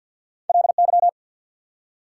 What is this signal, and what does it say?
Morse code, heard as a keyed tone, about 48 wpm, 690 Hz. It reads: LY
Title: Prefixes and Suffixes 50wpm